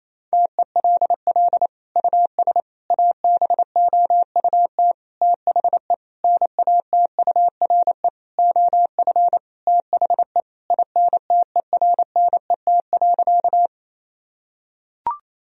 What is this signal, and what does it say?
Morse code, heard as a keyed tone, about 28 wpm, 700 Hz. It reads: TELL US ABOUT THE NATURE OF THE INTERNET.
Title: Sentences from Top 1000 Words - Review Part 4 - 28wpm